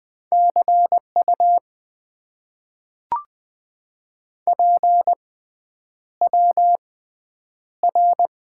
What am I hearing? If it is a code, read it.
Morse code, heard as a keyed tone, about 20 wpm, spaced as for 9 wpm, 700 Hz. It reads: CU PWR
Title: Top 90 QSO Elements - 6x Character Spacing - 20wpm